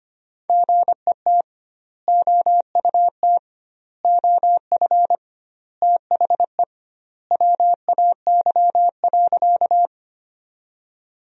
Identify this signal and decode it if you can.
Morse code, heard as a keyed tone, about 25 wpm, 700 Hz. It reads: GET OUT OF THE WAY.
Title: 2x Word-Spacing - Sentences from Top 200 Words - 25wpm